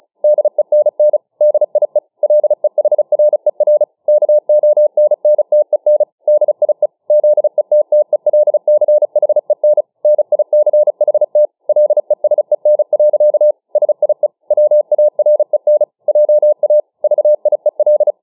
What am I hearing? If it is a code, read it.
Morse code, heard as a keyed tone, about 35 wpm, 600 Hz. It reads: DENN DIE LEHRER KONNTEN DIE ZETTELCHEN NICHT LESEN. SIE WAREN JA VIEL